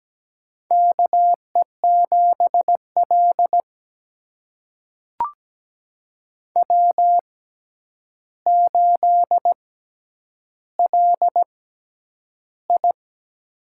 Morse code, heard as a keyed tone, about 17 wpm, spaced as for 9 wpm, 700 Hz. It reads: KE7L W8LI
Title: USA Call Signs - 4 Characters - 6x Character Spacing - 17wpm